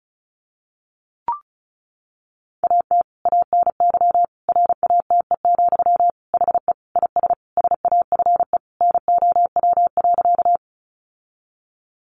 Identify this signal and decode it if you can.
Morse code, heard as a keyed tone, about 34 wpm, 700 Hz. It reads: AT ANY RATE, HE IS SAFE NOW.
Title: Sentences from Top 900 Words 35wpm